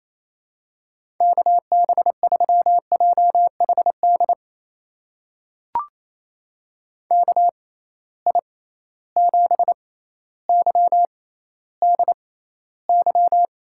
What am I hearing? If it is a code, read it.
Morse code, heard as a keyed tone, about 28 wpm, spaced as for 14 wpm, 700 Hz. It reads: KB3JHD KI7YDY
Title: USA Call Signs - 6 Characters - 6x Character Spacing - 28wpm